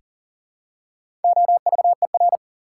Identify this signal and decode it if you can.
Morse code, heard as a keyed tone, about 40 wpm, 700 Hz. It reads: OVER
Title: Prefixes and Suffixes 40wpm